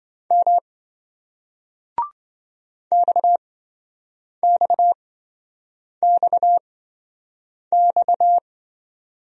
Morse code, heard as a keyed tone, about 27 wpm, spaced as for 7 wpm, 700 Hz. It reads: M XXXX